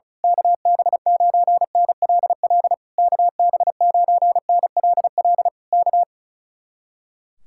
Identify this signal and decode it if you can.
Morse code, heard as a keyed tone, about 33 wpm, 700 Hz. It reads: KB9NLL KB9NLL K
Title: Speed-Racing Common QSO Phrases - 35wpm